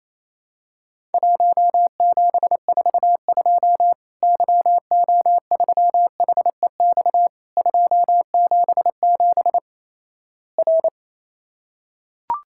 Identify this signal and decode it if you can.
Morse code, heard as a keyed tone, about 28 wpm, 700 Hz. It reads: 1742 YO3HEX 277 R
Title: Worked All Europe DX Contest - QTC Exchange - 28wpm